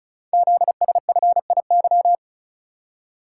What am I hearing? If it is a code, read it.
Morse code, heard as a keyed tone, about 36 wpm, 700 Hz. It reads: ZSFIY
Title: Code Groups - 5 Letters - 35wpm